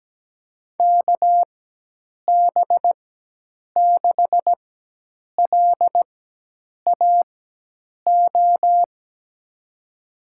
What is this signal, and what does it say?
Morse code, heard as a keyed tone, about 17 wpm, spaced as for 8 wpm, 700 Hz. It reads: KB6LAO
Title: USA Call Signs - 6 Characters - 4x Character Spacing - 17wpm